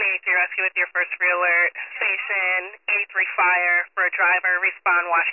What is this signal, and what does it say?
Emergency Alert System (SAME) tones, heard as harsh buzzing bursts, with speech. It is heard from 0.7 to 4.2 s.